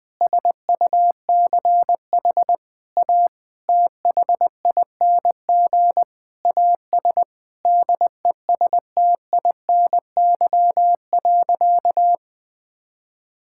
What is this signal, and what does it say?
Morse code, heard as a keyed tone, about 20 wpm, 700 Hz. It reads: SUCH A THING AS DESTINY.
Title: Sentences with 6-Word Phrases - 20wpm